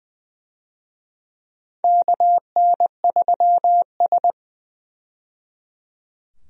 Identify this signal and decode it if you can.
Morse code, heard as a keyed tone, about 20 wpm, 700 Hz. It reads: KN3S
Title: USA Call Signs - 4 Characters - 20wpm